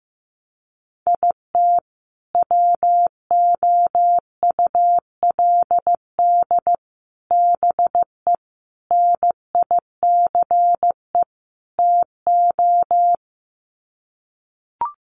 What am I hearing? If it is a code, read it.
Morse code, heard as a keyed tone, about 15 wpm, 700 Hz. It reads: IT WOULD BE NICE TO
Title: Sets of 5 Words Encore2 15wpm